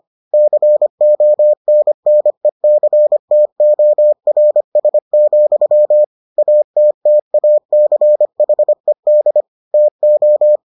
Morse code, heard as a keyed tone, about 25 wpm, 600 Hz. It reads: CONNECTORS, ATTACHED TO